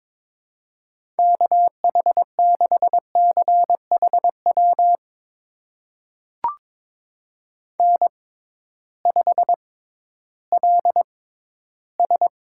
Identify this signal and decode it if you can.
Morse code, heard as a keyed tone, about 21 wpm, spaced as for 13 wpm, 700 Hz. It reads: KH6CHW N5LS
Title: USA Call Signs - 6x Character Spacing - 22wpm